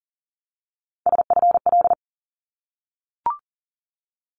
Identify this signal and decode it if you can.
Morse code, heard as a keyed tone, about 40 wpm, 700 Hz. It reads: SFL
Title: ARRL Sections 40wpm